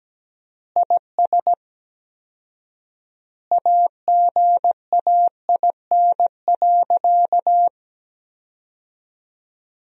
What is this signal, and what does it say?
Morse code, heard as a keyed tone, about 17 wpm, 700 Hz. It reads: IS AGAIN.
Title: 4x Word-Spacing - Sentences from Top 200 Words - 17wpm